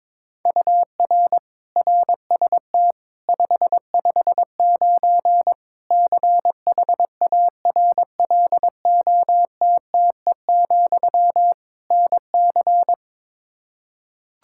Morse code, heard as a keyed tone, about 22 wpm, 700 Hz. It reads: UR RST 559 CHARLOTTE, NC
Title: Practice QSO - Exchange 1 - Beginning - 22wpm